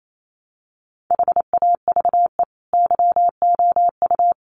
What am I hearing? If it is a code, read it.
Morse code, heard as a keyed tone, about 28 wpm, 700 Hz. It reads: HAVE YOU
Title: Sentences from Top 1100 Words 28wpm